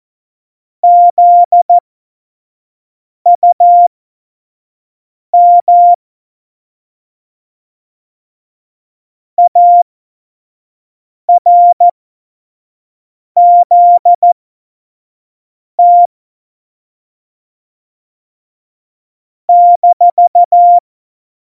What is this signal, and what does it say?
Morse code, heard as a keyed tone, about 14 wpm, spaced as for 5 wpm, 700 Hz. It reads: ZUM ARZT -
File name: Morsekurs / Witze / Witz 27 March 2018, 25 F70